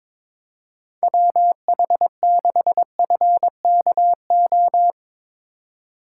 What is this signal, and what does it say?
Morse code, heard as a keyed tone, about 22 wpm, 700 Hz. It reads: WH6FKO
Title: USA Call Signs - 6 Characters - 3x Character Spacing - 22wpm